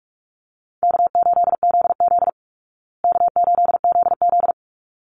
Morse code, heard as a keyed tone, about 44 wpm, 700 Hz. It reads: K8ZZ K8ZZ
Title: Easy Call Signs Repeated 2x - 45wpm